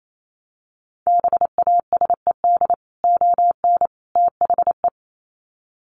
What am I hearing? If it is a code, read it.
Morse code, heard as a keyed tone, about 28 wpm, 700 Hz. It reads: BASED ON THE
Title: Sets of 3 Words 28wpm